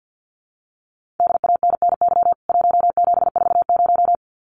Morse code, heard as a keyed tone, about 49 wpm, 700 Hz. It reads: DANNY 1740
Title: CWT Exchange 50wpm